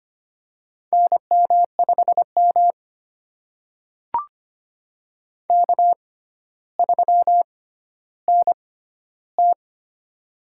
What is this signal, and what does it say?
Morse code, heard as a keyed tone, about 25 wpm, spaced as for 13 wpm, 700 Hz. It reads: NM5M K3NT
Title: USA Call Signs - 6x Character Spacing - 25wpm